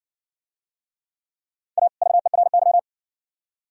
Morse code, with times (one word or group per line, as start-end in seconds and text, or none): 1.8-1.9 A
2.0-2.8 VERY